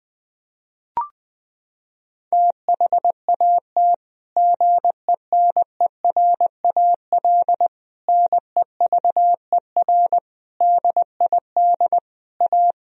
Morse code, heard as a keyed tone, about 20 wpm, 700 Hz. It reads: THAT GENERAL NEVER DID A
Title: Sentences from Top 1000 Words - Review Part 4 - 20wpm